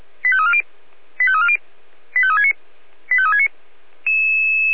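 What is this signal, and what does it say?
Five-tone selective call (ZVEI), heard as quick runs of five tones, tones of about 70 ms. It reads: 85340 85340 85369 85369
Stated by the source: ZVEI five-tone selective call received by SDR.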